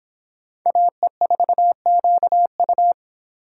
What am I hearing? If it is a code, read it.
Morse code, heard as a keyed tone, about 26 wpm, 700 Hz. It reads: AE4QU